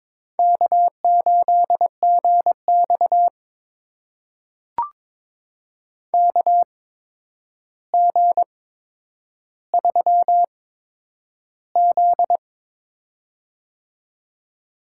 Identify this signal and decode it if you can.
Morse code, heard as a keyed tone, about 22 wpm, spaced as for 9 wpm, 700 Hz. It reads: K8GX KG3Z
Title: USA Call Signs - 8x Character Spacing - 22wpm